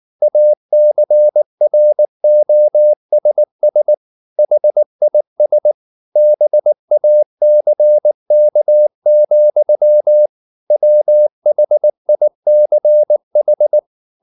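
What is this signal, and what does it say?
Morse code, heard as a keyed tone, about 19 wpm, 600 Hz. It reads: ACROSS HIS BACK, WHICH
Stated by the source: Morse code at 19 wpm from CW med Gnister, reading Ken Follett's novel Never in English.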